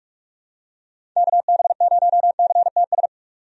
Morse code, heard as a keyed tone, about 46 wpm, 700 Hz. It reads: KB0CTS
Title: USA Call Signs - 6 Characters - 45wpm